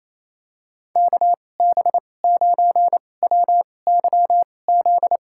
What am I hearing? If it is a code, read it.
Morse code, heard as a keyed tone, about 28 wpm, spaced as for 20 wpm, 690 Hz. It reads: KB9WYZ
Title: USA Call Signs - 2x Character Spacing - 28wpm